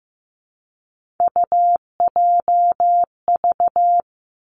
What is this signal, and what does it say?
Morse code, heard as a keyed tone, about 15 wpm, 700 Hz. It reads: UJV
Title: Three Challenging Letter-Number Combinations 15wpm